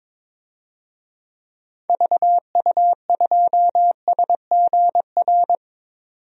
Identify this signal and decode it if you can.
Morse code, heard as a keyed tone, about 22 wpm, 700 Hz. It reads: VU2SGR
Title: DXCC-100 Call Signs - 6 Characters - 22wpm